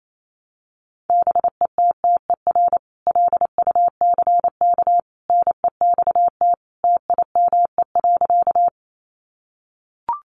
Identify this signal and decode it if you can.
Morse code, heard as a keyed tone, about 28 wpm, 700 Hz. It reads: BETTER LUCK NEXT TIME.
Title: Sentences from Top 1500 Words 28wpm